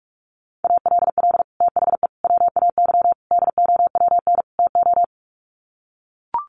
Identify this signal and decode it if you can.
Morse code, heard as a keyed tone, about 44 wpm, 700 Hz. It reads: ALL THE WAY DOWN TO
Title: Sets of 5 Words Encore2 45wpm